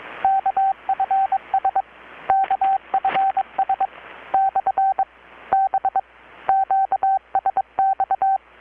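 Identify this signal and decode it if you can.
Morse code, heard as a keyed tone, about 21 wpm, 770 Hz, with radio noise. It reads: KFS KFS / B QSX
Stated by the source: Morse from KPH's Night of Nights XX maritime radiotelegraphy event, received in Chiapas, Mexico about 3,800 km away on a Kenwood R-600 with a grounded 10 m long-wire antenna.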